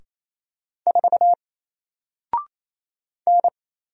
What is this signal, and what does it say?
Morse code, heard as a keyed tone, about 27 wpm, 700 Hz. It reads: 4 N
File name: Single-Letter-Number-28wpm